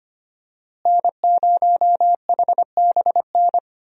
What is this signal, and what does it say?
Morse code, heard as a keyed tone, about 25 wpm, 700 Hz. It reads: N0HBN
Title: USA Call Signs - 5 Characters - Repeated 3x - 25wpm